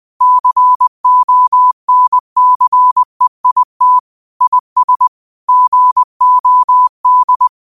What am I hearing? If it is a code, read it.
Morse code, heard as a keyed tone, about 20 wpm, 1000 Hz. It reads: CONCEIT IS GOD
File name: qotdfiles / 20wpm